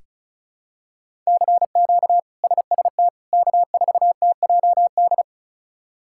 Morse code, heard as a keyed tone, about 36 wpm, 700 Hz. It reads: CQ SST K4TJD
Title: K1USN SST QSO Practice - 35wpm